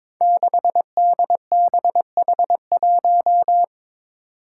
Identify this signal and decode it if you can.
Morse code, heard as a keyed tone, about 23 wpm, 700 Hz. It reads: 6DBH1
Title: Five Challenging Letter-Number Combinations 22wpm